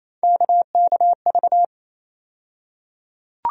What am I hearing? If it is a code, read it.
Morse code, heard as a keyed tone, about 28 wpm, 700 Hz. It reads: KKV
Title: USA Call Sign Postfixes - 3 Letters 28wpm